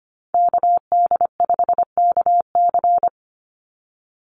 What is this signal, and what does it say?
Morse code, heard as a keyed tone, about 25 wpm, 700 Hz. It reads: KD5KC
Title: NA SOTA Activator Callsigns 25wpm